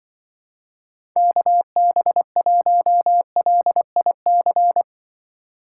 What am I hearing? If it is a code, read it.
Morse code, heard as a keyed tone, about 24 wpm, 700 Hz. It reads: KB1LIC
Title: Speed Racing - USA 2x3 Call Signs 20wpm